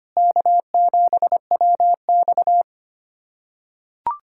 Morse code, heard as a keyed tone, about 25 wpm, 700 Hz. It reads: K7WX
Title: USA Callsign DE Callsign - 4 Characters - 25wpm